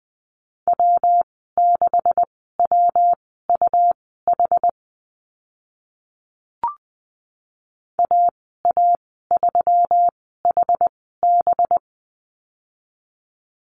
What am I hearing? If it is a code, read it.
Morse code, heard as a keyed tone, about 20 wpm, spaced as for 14 wpm, 700 Hz. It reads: W6WUH AA3HB
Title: 193 Lesson 31g - Letter v - Call Signs 20wpm